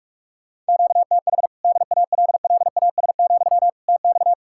Morse code, heard as a keyed tone, about 59 wpm, spaced as for 52 wpm, 700 Hz. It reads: QTH DALLAS, TX
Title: Speed-Racing Common QSO Phrases - 45wpm